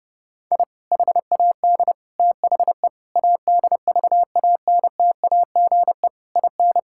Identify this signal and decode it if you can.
Morse code, heard as a keyed tone, about 30 wpm, 700 Hz. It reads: I HAD THE ADVANTAGE IN
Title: Sentences from Top 1000 Words - Review Part 6 - 30wpm